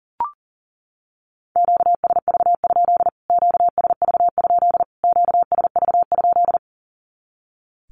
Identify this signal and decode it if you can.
Morse code, heard as a keyed tone, about 40 wpm, 700 Hz. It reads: QSV? QSV? QSV?
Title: Q-Codes Repeated 3x 40wpm